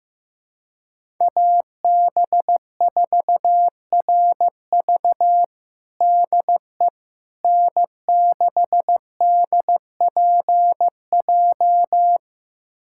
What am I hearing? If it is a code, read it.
Morse code, heard as a keyed tone, about 16 wpm, 700 Hz. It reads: AB4RV DE N6DPJ
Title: USA Callsign DE Callsign - 5 Characters - 15wpm